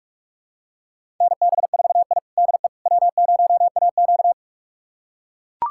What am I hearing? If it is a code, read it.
Morse code, heard as a keyed tone, about 44 wpm, 690 Hz. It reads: NB4I DE W0AQ
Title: USA Callsign DE Callsign - 4 Characters - 45wpm